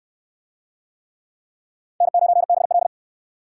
Morse code, heard as a keyed tone, about 69 wpm, 700 Hz. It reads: N9BL